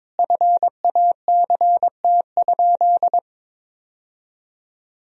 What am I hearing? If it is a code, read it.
Morse code, heard as a keyed tone, about 22 wpm, 700 Hz. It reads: FACT?
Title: Sentences from Top 1000 Words - Review Part 3 - 22wpm